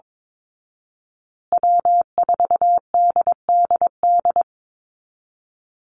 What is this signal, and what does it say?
Morse code, heard as a keyed tone, about 22 wpm, 700 Hz. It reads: W4DDD
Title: Call Signs Repeated 2x - 22wpm